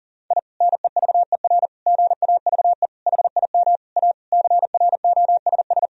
Morse code, heard as a keyed tone, about 40 wpm, 700 Hz. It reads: I NEVER GAVE HIM A CROSS
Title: Sentences from Top 1000 Words - Review Part 1 - 40wpm